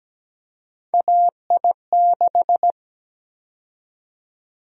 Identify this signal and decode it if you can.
Morse code, heard as a keyed tone, about 17 wpm, 700 Hz. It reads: AI6
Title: USA Call Sign Prefixes - 17wpm